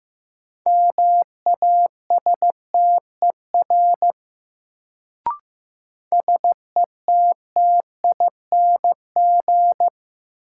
Morse code, heard as a keyed tone, about 16 wpm, 700 Hz. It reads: MASTER SETTING
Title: Sentences from Top 1300 Words - New Word Drill - 15wpm